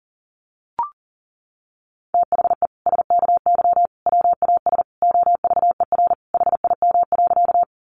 Morse code, heard as a keyed tone, about 40 wpm, 700 Hz. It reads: THE SKY WAS OVER HIM.
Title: Sentences from Top 1200 Words 40wpm